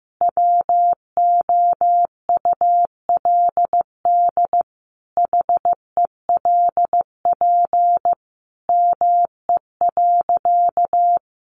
Morse code, heard as a keyed tone, about 15 wpm, 700 Hz. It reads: WOULD HELP ME.
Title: Sentences from Top 1300 Words 15wpm